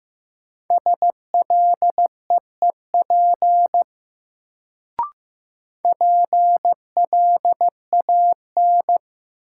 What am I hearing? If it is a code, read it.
Morse code, heard as a keyed tone, about 15 wpm, 700 Hz. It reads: SLEEP PLAN
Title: Sentences from Top 500 Words - New Word Drill - 15wpm